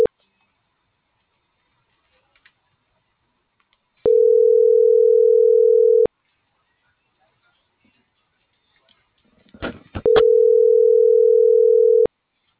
Ringback tone (North America), 440 + 480 Hz.
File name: CENTRAL TERRESTRE / out alberto.lazar.25 - .42737